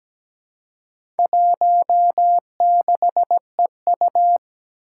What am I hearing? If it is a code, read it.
Morse code, heard as a keyed tone, about 17 wpm, 700 Hz. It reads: 16EU
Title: Four Challenging Letter-Number Combinations 17wpm